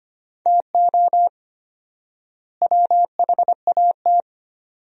Morse code, heard as a keyed tone, about 25 wpm, 700 Hz. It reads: TO WHAT
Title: 4x Word-Spacing - Sentences from Top 200 Words - 25wpm